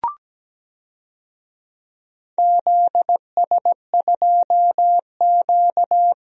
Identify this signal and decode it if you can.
Morse code, heard as a keyed tone, about 17 wpm, 700 Hz. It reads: ZS2Q
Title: DXCC-100 Call Signs - 4 Characters - Repeated 3x - 17wpm